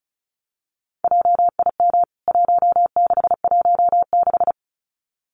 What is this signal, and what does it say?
Morse code, heard as a keyed tone, about 34 wpm, 700 Hz. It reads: JIM 1616
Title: CWT Exchange 35wpm